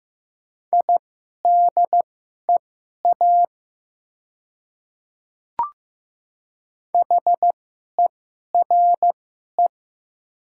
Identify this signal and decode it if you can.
Morse code, heard as a keyed tone, about 15 wpm, spaced as for 11 wpm, 700 Hz. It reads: IDEA HERE
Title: 2x Character Spacing - Top 300 Words - 15wpm